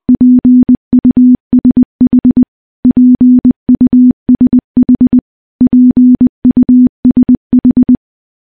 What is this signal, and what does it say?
Morse code, heard as a keyed tone, about 20 wpm, 250 Hz. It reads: PUSH PUSH PUSH
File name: morse-training-20WPM